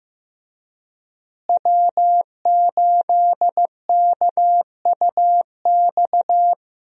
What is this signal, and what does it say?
Morse code, heard as a keyed tone, about 15 wpm, 700 Hz. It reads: W8KUX
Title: USA Call Signs - 5 Characters - Repeated 2x - 15wpm